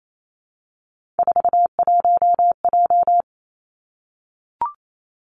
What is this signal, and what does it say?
Morse code, heard as a keyed tone, about 28 wpm, 700 Hz. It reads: 41J
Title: Three Challenging Letter-Number Combinations 28wpm